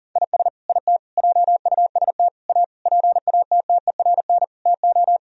Morse code, heard as a keyed tone, about 40 wpm, 700 Hz. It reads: IS IT JUST A PATTERN TO
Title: Sentences from Top 1000 Words - Review Part 2 - 40wpm